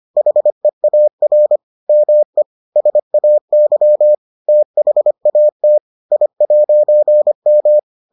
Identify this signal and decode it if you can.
Morse code, heard as a keyed tone, about 25 wpm, 600 Hz. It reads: HEAR ME SAY THAT I'M